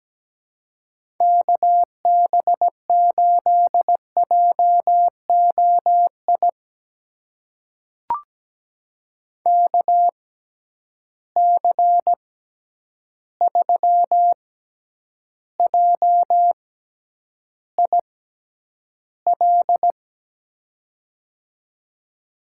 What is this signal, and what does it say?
Morse code, heard as a keyed tone, about 17 wpm, spaced as for 9 wpm, 700 Hz. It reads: KB8JOI KC3JIL
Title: USA Call Signs - 6 Characters - 6x Character Spacing - 17wpm